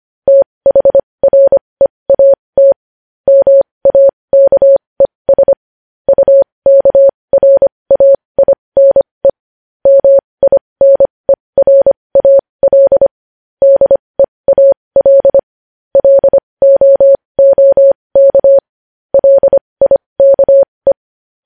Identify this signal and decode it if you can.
Morse code, heard as a keyed tone, about 25 wpm, spaced as for 20 wpm, 550 Hz. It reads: THREAT MAKES UKRAINE MINERAL DEAL LOOK LIKE
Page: news-headlines-in-morse-code-at-20-wpm